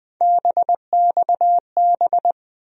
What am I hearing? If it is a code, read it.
Morse code, heard as a keyed tone, about 20 wpm, 700 Hz. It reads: BXB
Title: USA Call Sign Postfixes - 3 Letters 20wpm